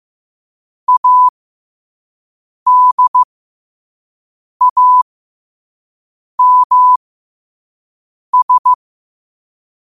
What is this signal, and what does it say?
Morse code, heard as a keyed tone, about 15 wpm, spaced as for 5 wpm, 1000 Hz. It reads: ADAMS
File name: qotdfiles / 5wpm